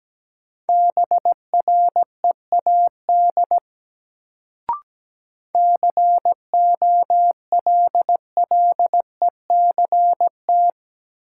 Morse code, heard as a keyed tone, about 17 wpm, 700 Hz. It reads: BREAD COLLECT
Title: Sentences from Top 1600 Words - New Word Drill Alt - 17wpm